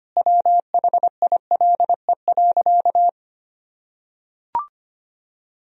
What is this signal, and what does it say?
Morse code, heard as a keyed tone, about 25 wpm, 700 Hz. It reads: WHILE.
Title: Sentences from Top 1000 Words - Review Part 1 - 25wpm